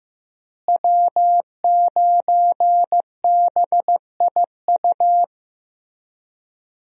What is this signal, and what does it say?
Morse code, heard as a keyed tone, about 15 wpm, 700 Hz. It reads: W9BIU
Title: USA Call Signs - 5 Characters - Repeated 3x - 15wpm